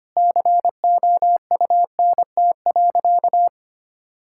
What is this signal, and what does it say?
Morse code, heard as a keyed tone, about 25 wpm, 700 Hz. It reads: COUNT.
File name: practice / 25wpm / Difficult Sentences with Hints - 25wpm